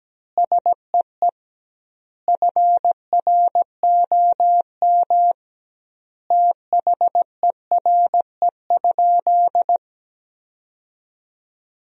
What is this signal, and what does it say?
Morse code, heard as a keyed tone, about 17 wpm, 700 Hz. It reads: SEE FROM THERE?
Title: 2x Word-Spacing - Sentences from Top 100 Words - 17wpm